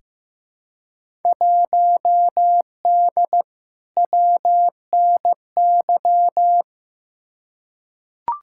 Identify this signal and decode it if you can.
Morse code, heard as a keyed tone, about 15 wpm, 700 Hz. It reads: 1D WNY
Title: Field Day Exchanges - 15wpm